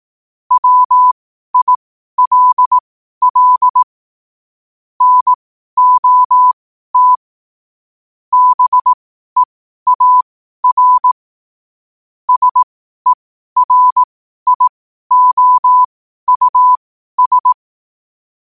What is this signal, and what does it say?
Morse code, heard as a keyed tone, about 18 wpm, spaced as for 12 wpm, 1000 Hz. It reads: WILL NOT BEAR SERIOUS
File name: qotdfiles / 12wpm